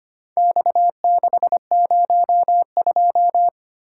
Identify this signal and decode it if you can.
Morse code, heard as a keyed tone, about 25 wpm, 700 Hz. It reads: X602
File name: practice / 25wpm / Four Letter-Number Combinations - 25wpm